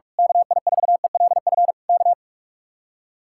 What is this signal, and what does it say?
Morse code, heard as a keyed tone, about 46 wpm, 700 Hz. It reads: KI4ELF K
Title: Speed-Racing Common QSO Phrases - 45wpm